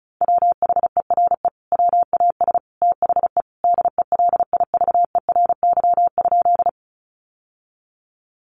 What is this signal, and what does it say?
Morse code, heard as a keyed tone, about 34 wpm, 700 Hz. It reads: WHERE WAS THE DELIVERY?
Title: Sentences from Top 1500 Words 35wpm